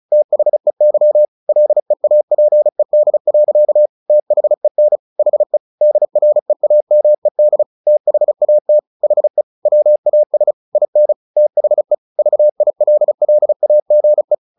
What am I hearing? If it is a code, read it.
Morse code, heard as a keyed tone, about 34 wpm, 600 Hz. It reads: THEY LEAPED. THEN HE DREAMED THAT HE WAS IN THE VILLAGE
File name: the old man and the sea 2 35wpm